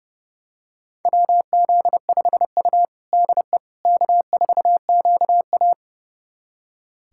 Morse code, heard as a keyed tone, about 30 wpm, 700 Hz. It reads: WZ5U DE K4QA
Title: USA Callsign DE Callsign - 4 Characters - 30wpm